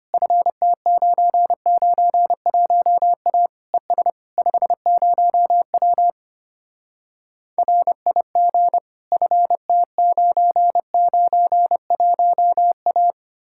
Morse code, heard as a keyed tone, about 33 wpm, 700 Hz. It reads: FT991A ES 50W RIG FT991A